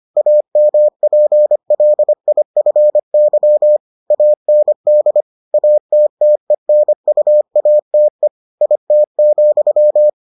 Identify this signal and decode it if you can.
Morse code, heard as a keyed tone, about 25 wpm, 600 Hz. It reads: AMPLIFY AND ATTENUATE IT,